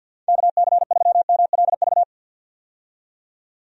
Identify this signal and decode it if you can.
Morse code, heard as a keyed tone, about 53 wpm, 700 Hz. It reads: KC3MLV
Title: USA Call Signs - 6 Characters - 50wpm